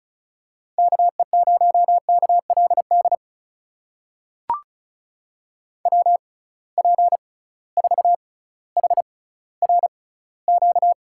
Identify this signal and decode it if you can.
Morse code, heard as a keyed tone, about 34 wpm, spaced as for 18 wpm, 700 Hz. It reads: KE0KLD WP4HRQ
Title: USA Call Signs - 6 Characters - 6x Character Spacing - 35wpm